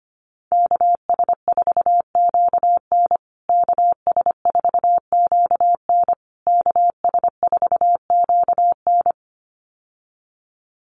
Morse code, heard as a keyed tone, about 25 wpm, 700 Hz. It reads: KS4QN KS4QN KS4QN
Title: USA Call Signs Repeated 3x 25wpm